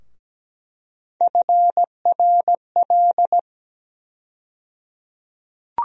Morse code, heard as a keyed tone, about 17 wpm, 700 Hz. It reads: FRL
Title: USA Call Sign Postfixes - 3 Letters 17wpm